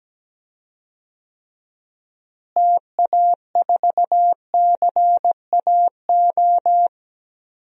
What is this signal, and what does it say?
Morse code, heard as a keyed tone, about 17 wpm, 700 Hz. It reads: TA4CAO
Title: DXCC-100 Call Signs - 6 Characters - Repeated 2x - 17wpm